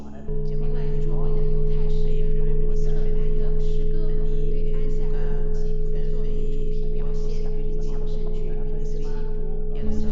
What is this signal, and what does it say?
Dial tone (Europe, 425 Hz), with speech.